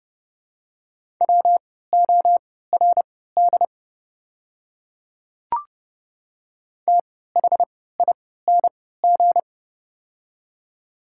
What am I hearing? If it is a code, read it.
Morse code, heard as a keyed tone, about 30 wpm, spaced as for 17 wpm, 700 Hz. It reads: WORD THING
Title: 3x Character Spacing - Top 100 Words - 30wpm